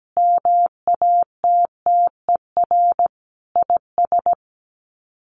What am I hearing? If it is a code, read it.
Morse code, heard as a keyed tone, about 17 wpm, 700 Hz. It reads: MATTER IS
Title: Sets of 5 Words 17wpm